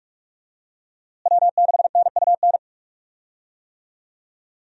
Morse code, heard as a keyed tone, about 46 wpm, 700 Hz. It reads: W6NUN
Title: USA Call Signs - 5 Characters - Repeated 2x - 45wpm